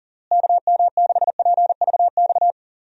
Morse code, heard as a keyed tone, about 37 wpm, 700 Hz. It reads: KM6PVX
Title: USA Call Signs - 6 Characters - Repeated 3x - 40wpm